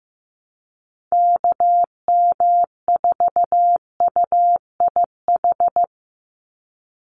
Morse code, heard as a keyed tone, about 15 wpm, 700 Hz. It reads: KM4UIH
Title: USA Call Signs 15wpm